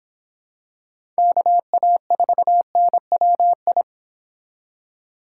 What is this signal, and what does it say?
Morse code, heard as a keyed tone, about 26 wpm, 700 Hz. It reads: KA4NWI